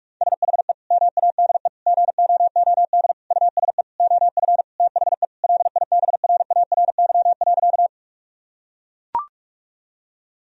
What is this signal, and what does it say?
Morse code, heard as a keyed tone, about 41 wpm, 700 Hz. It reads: SHE MADE GOOD USE OF THE LIBRARY.
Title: Sentences from Top 1000 Words - Review Part 4 - 45wpm